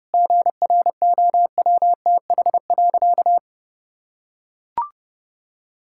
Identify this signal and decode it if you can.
Morse code, heard as a keyed tone, about 30 wpm, 700 Hz. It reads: GROWTH.
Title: Sentences from Top 1000 Words - Review Part 5 - 30wpm